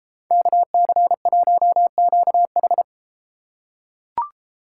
Morse code, heard as a keyed tone, about 31 wpm, 700 Hz. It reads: KC1QH